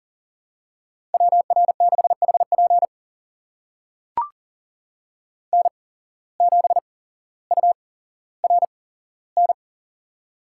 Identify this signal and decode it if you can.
Morse code, heard as a keyed tone, about 40 wpm, spaced as for 17 wpm, 700 Hz. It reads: WR6HP N7URN
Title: USA Call Signs - 5 Characters - 8x Character Spacing - 40wpm